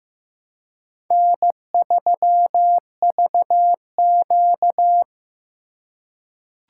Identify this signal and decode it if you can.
Morse code, heard as a keyed tone, about 15 wpm, 700 Hz. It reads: N3VQ
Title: USA Call Signs - 4 Characters - 3x Character Spacing - 15wpm